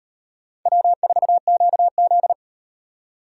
Morse code, heard as a keyed tone, about 37 wpm, 700 Hz. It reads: W4QZ